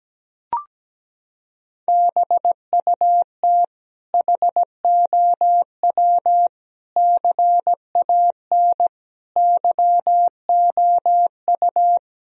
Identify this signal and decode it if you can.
Morse code, heard as a keyed tone, about 17 wpm, 700 Hz. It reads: BUT HOW CAN YOU
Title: Sentences from Top 1000 Words - Review Part 2 - 17wpm